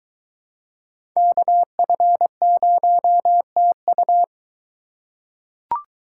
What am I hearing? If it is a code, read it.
Morse code, heard as a keyed tone, about 23 wpm, 700 Hz. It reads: KF0TU